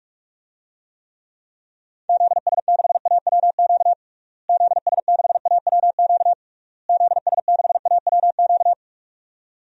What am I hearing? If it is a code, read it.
Morse code, heard as a keyed tone, about 46 wpm, 700 Hz. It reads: ZS6AWQ ZS6AWQ ZS6AWQ
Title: DXCC-100 Call Signs - 6 Characters - Repeated 3x - 45wpm